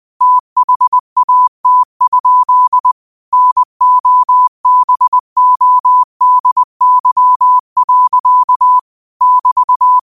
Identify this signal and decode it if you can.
Morse code, heard as a keyed tone, about 20 wpm, 1000 Hz. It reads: THAT? NOBODY. <BT>